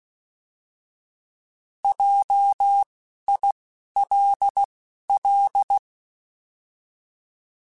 Morse code, heard as a keyed tone, about 16 wpm, spaced as for 11 wpm, 800 Hz. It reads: JILL
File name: Alphabetical / Alphabetical mp3s / Js @10